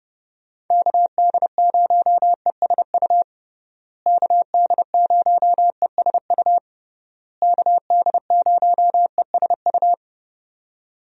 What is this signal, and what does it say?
Morse code, heard as a keyed tone, about 32 wpm, 700 Hz. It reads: KD0ESU KD0ESU KD0ESU
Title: USA Call Signs - 6 Characters - Repeated 3x - 30wpm